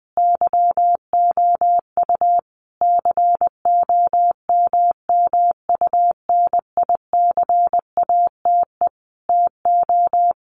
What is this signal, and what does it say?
Morse code, heard as a keyed tone, about 19 wpm, 700 Hz. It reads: YOU COMMUNICATE TO
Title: Sentences from Top 1300 Words 20wpm